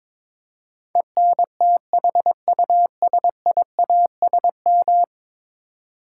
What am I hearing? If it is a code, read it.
Morse code, heard as a keyed tone, about 22 wpm, 700 Hz. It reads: ENTHUSIASM
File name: practice / 22wpm / 10 Letter Words - First 3 Letters Given - 22wpm